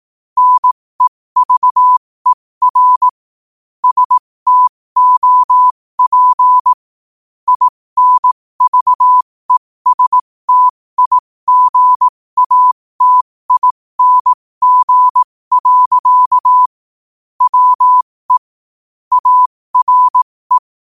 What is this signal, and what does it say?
Morse code, heard as a keyed tone, about 18 wpm, spaced as for 15 wpm, 1000 Hz. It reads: NEVER STOP INVESTIGATING. WE ARE
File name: qotdfiles / 15wpm